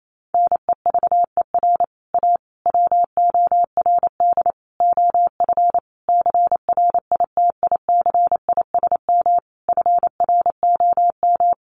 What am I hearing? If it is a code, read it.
Morse code, heard as a keyed tone, about 28 wpm, 700 Hz. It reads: NEVER A WORD OF CRITICISM FROM
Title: Sentences from Top 1300 Words 28wpm